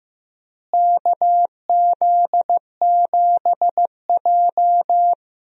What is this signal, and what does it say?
Morse code, heard as a keyed tone, about 15 wpm, 700 Hz. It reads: KZ7J
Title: USA Call Signs - 4 Characters - Repeated 3x - 15wpm